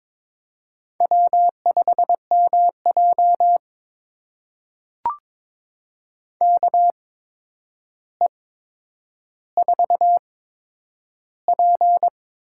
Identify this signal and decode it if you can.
Morse code, heard as a keyed tone, about 22 wpm, spaced as for 9 wpm, 700 Hz. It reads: W5MJ KE4P
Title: USA Call Signs - 4 Characters - 8x Character Spacing - 22wpm